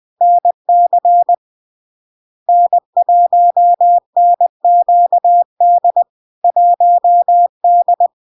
Morse code, heard as a keyed tone, about 20 wpm, 700 Hz. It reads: NC N1NQD 1D
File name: LARK-FD-2022-Log-in-CW-20wpm-700-Hz